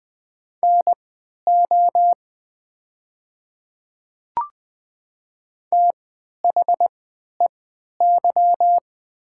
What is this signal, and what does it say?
Morse code, heard as a keyed tone, about 20 wpm, spaced as for 11 wpm, 700 Hz. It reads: NO THEY
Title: 3x Character Spacing - Top 200 Words - 20wpm